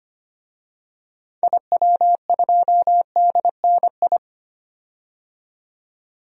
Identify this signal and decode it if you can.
Morse code, heard as a keyed tone, about 25 wpm, 700 Hz. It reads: IW2DNI
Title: DXCC-100 Call Signs - 6 Characters - 25wpm